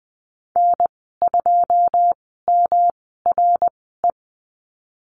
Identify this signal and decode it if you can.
Morse code, heard as a keyed tone, about 20 wpm, spaced as for 15 wpm, 700 Hz. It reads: N2MRE
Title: 193 Lesson 31g - Letter v - Call Signs 20wpm